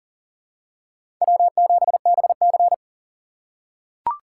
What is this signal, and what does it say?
Morse code, heard as a keyed tone, about 40 wpm, 700 Hz. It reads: W7BC
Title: USA Call Signs - 4 Characters - 3x Character Spacing - 40wpm